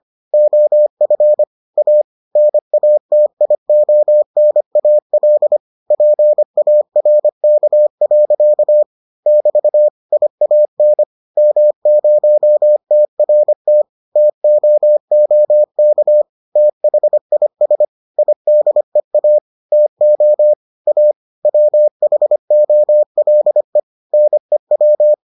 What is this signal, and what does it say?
Morse code, heard as a keyed tone, about 25 wpm, 600 Hz. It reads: OF A NATIONAL PARK. <BT> IAN M0TRT TOOK THIS IDEA TO A WHOLE NEW